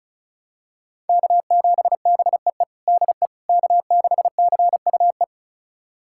Morse code, heard as a keyed tone, about 36 wpm, 700 Hz. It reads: K7BEE DE K6CUE
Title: USA Callsign DE Callsign - 5 Characters - 35wpm